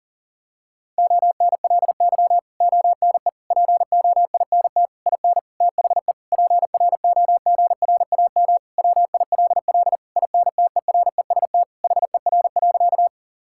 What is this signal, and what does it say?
Morse code, heard as a keyed tone, about 40 wpm, 700 Hz. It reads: ONLY ONE POINT IN THE PROGRAM WILL INTEREST HER.
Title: Sentences from Top 1000 Words - Review Part 5 - 40wpm